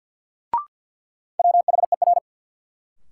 Morse code, heard as a keyed tone, about 48 wpm, 700 Hz. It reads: WHER
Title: Sentences from Top 1500 Words - New Word Drill Alt - 50wpm